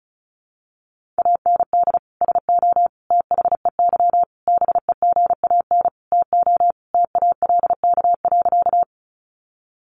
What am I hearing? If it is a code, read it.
Morse code, heard as a keyed tone, about 34 wpm, 700 Hz. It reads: AND SO THEY BEGAN TO TALK.
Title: Sentences from Top 400 Words 35wpm